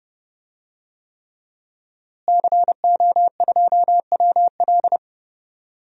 Morse code, heard as a keyed tone, about 30 wpm, 700 Hz. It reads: CO2WL
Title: DXCC-100 Call Signs - 5 Characters - Repeated 3x - 30wpm